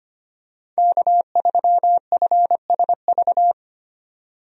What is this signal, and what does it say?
Morse code, heard as a keyed tone, about 25 wpm, 700 Hz. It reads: K3FSV
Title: USA Call Signs - 5 Characters - 25wpm